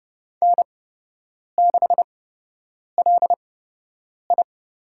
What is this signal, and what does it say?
Morse code, heard as a keyed tone, about 30 wpm, spaced as for 8 wpm, 700 Hz. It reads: N6LI
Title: USA Call Signs - 8x Character Spacing - 30wpm